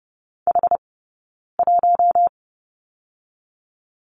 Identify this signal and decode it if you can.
Morse code, heard as a keyed tone, about 30 wpm, 700 Hz. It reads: H 1
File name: course / 30:10wpm / 099 Lesson 17d - Letter u - Two Characters 30wpm